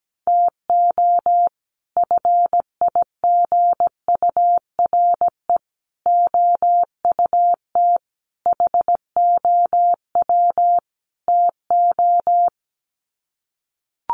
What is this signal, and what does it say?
Morse code, heard as a keyed tone, about 17 wpm, 700 Hz. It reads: TO FIGURE OUT HOW TO
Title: Sets of 5 Words 17wpm